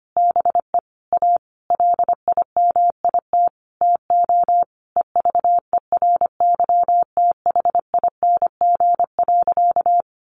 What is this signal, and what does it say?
Morse code, heard as a keyed tone, about 25 wpm, 700 Hz. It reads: BE A LIMIT TO EVERYTHING.
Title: Sentences from Top 1500 Words 25wpm